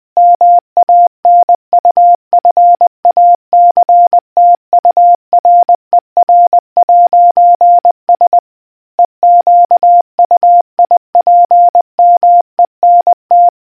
Morse code, heard as a keyed tone, about 20 wpm, 700 Hz. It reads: MANUFACTURER'S EQUIPMENT